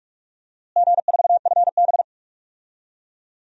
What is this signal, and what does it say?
Morse code, heard as a keyed tone, about 46 wpm, 700 Hz. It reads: G4FB
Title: DXCC-100 Call Signs - 4 Characters - Repeated 2x - 45wpm